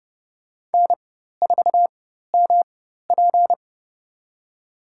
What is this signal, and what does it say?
Morse code, heard as a keyed tone, about 30 wpm, spaced as for 14 wpm, 700 Hz. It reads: N4MP
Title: USA Call Signs - 4x Character Spacing - 30wpm